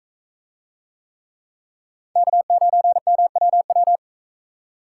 Morse code, heard as a keyed tone, about 42 wpm, 700 Hz. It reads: K9MWW